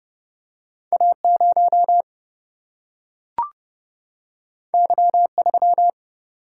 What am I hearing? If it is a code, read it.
Morse code, heard as a keyed tone, about 30 wpm, 700 Hz. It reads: A0 Y3
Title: Single-Letter Single-Number 30wpm